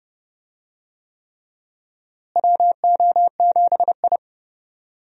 Morse code, heard as a keyed tone, about 30 wpm, 700 Hz. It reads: WO7I
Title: USA Call Signs - 4 Characters - Repeated 3x - 30wpm